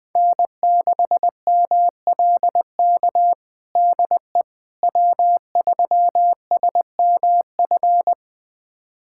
Morse code, heard as a keyed tone, about 21 wpm, 700 Hz. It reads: N6MLK DE W3SMF
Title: USA Callsign DE Callsign - 5 Characters - 20wpm